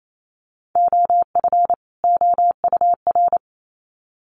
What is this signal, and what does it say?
Morse code, heard as a keyed tone, about 28 wpm, 700 Hz. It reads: OF OUR
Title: Sets of 2 Words 28wpm